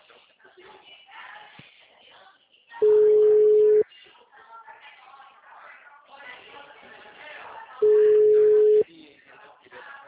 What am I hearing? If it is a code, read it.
Ringback tone (Europe, 425 Hz), with speech.